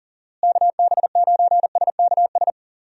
Morse code, heard as a keyed tone, about 40 wpm, 700 Hz. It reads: KB9SKS
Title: USA Call Signs - 6 Characters - Repeated 2x - 40wpm